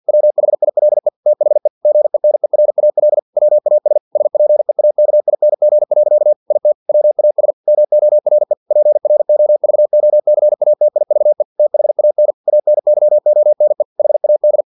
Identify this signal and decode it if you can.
Morse code, heard as a keyed tone, about 50 wpm, 600 Hz. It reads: WHILE THE GENERAL WAS SPEAKING. IT WAS MORE PROVOCATIVE THAN ANYONE HAD
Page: Never 49wpm